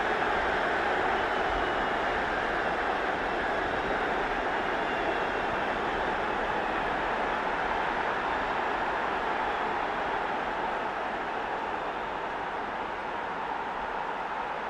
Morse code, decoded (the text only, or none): none